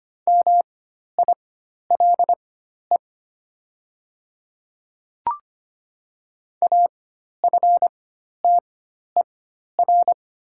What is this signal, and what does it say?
Morse code, heard as a keyed tone, about 25 wpm, spaced as for 12 wpm, 700 Hz. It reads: MILE AFTER
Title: 4x Character Spacing - Top 500 Words - 25wpm